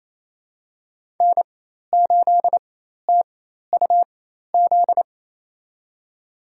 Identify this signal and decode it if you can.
Morse code, heard as a keyed tone, about 28 wpm, spaced as for 13 wpm, 700 Hz. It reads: N8TUZ
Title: USA Call Signs - 4x Character Spacing - 28wpm